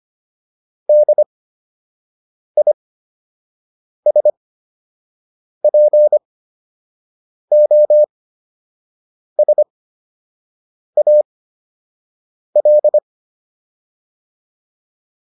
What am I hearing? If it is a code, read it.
Morse code, heard as a keyed tone, about 25 wpm, spaced as for 6 wpm, 600 Hz. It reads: DISPOSAL